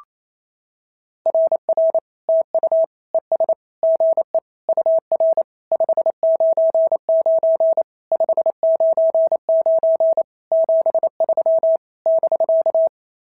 Morse code, heard as a keyed tone, about 27 wpm, 650 Hz. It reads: RR TU ES GE UR 599 599 73 <BK>
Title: SOTA QSO Practice 28wpm